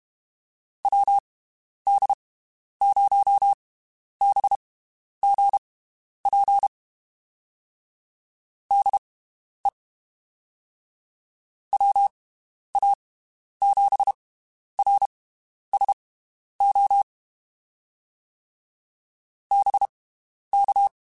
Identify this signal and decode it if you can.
Morse code, heard as a keyed tone, about 32 wpm, spaced as for 11 wpm, 800 Hz. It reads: WD0BGP DE WA7RSO BK
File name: (spaced) / QSOs Spaced-mp3s / WD0BGP 1 @30 sp